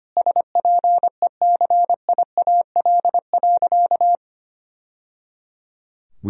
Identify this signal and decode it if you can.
Morse code, heard as a keyed tone, about 25 wpm, 700 Hz. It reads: SPECIAL.
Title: Sentences from Top 1000 Words - Review Part 3 - 25wpm